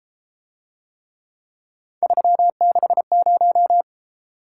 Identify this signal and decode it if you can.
Morse code, heard as a keyed tone, about 32 wpm, 700 Hz. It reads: 360